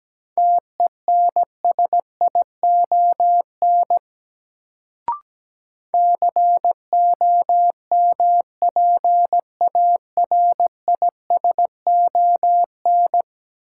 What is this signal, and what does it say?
Morse code, heard as a keyed tone, about 17 wpm, 700 Hz. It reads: TENSION COMPARISON
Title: Sentences from Top 1600 Words - New Word Drill Alt - 17wpm